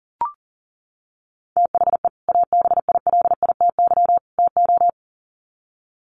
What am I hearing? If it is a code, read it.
Morse code, heard as a keyed tone, about 40 wpm, 700 Hz. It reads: THE ABILITY TO
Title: Sets of 3 Words 40wpm